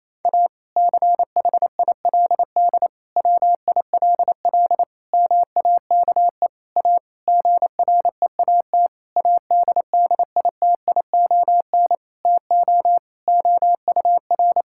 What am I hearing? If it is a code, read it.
Morse code, heard as a keyed tone, about 28 wpm, 700 Hz. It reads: A CHILD WILL MAKE A GREAT ADDITION TO OUR
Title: Sentences from Top 1000 Words - Review Part 6 - 28wpm